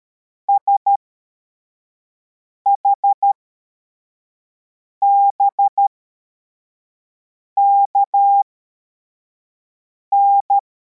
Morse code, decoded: SHBKN